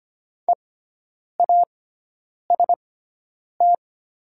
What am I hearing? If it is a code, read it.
Morse code, heard as a keyed tone, about 25 wpm, spaced as for 9 wpm, 700 Hz. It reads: EAST
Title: 6x Character Spacing - Top 500 Words - 25wpm